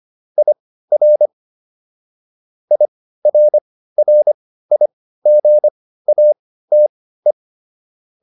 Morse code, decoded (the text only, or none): IR IRRIGATE